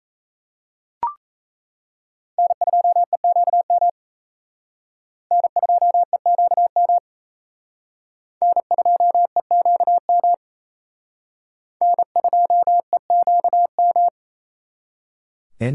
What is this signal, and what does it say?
Morse code, heard as a keyed tone, about 35 wpm, 700 Hz. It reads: N2EQM N2EQM N2EQM N2EQM
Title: Speed Racing - USA 1x3 Call Signs 28wpm